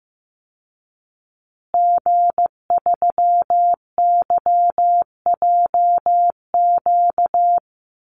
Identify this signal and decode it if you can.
Morse code, heard as a keyed tone, about 15 wpm, 700 Hz. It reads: G3YJQ
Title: Call Signs Repeated 3x - 15wpm